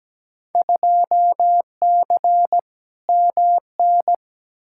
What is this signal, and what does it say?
Morse code, heard as a keyed tone, about 17 wpm, 700 Hz. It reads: 2C MN
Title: Field Day Exchanges Encore - 17wpm